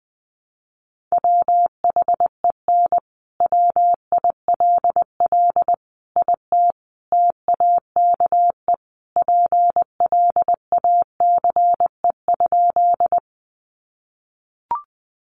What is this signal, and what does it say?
Morse code, heard as a keyed tone, about 20 wpm, 700 Hz. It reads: WHEN WILL IT TAKE PLACE?
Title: Sentences from Top 200 Words 20wpm